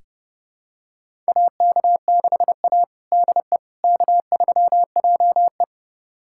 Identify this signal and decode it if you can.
Morse code, heard as a keyed tone, about 31 wpm, 700 Hz. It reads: AK6A DE K3JE
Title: USA Callsign DE Callsign - 4 Characters - 30wpm